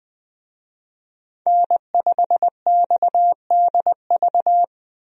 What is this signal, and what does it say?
Morse code, heard as a keyed tone, about 20 wpm, 700 Hz. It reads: N5XDV
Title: USA Call Signs - 5 Characters - Repeated 2x - 20wpm